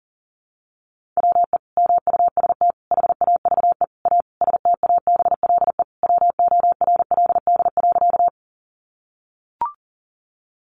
Morse code, heard as a keyed tone, about 37 wpm, 700 Hz. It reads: WE MUST HAVE A STABLE WORLD.
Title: Sentences from Top 1500 Words 40wpm